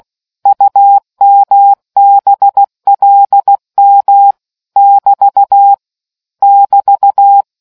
Morse code, heard as a keyed tone, about 16 wpm, 800 Hz. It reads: UMBLM <BT> <BT>